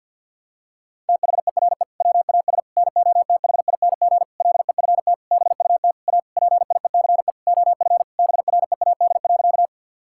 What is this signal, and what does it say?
Morse code, heard as a keyed tone, about 50 wpm, 700 Hz. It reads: THERE WAS NOTHING LEFT BUT A PIECE OF BREAD.
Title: Sentences from Top 1600 Words 50wpm